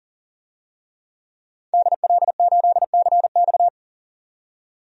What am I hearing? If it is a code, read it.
Morse code, heard as a keyed tone, about 37 wpm, 700 Hz. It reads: DL8CX
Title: DXCC-100 Call Signs - 5 Characters - 40wpm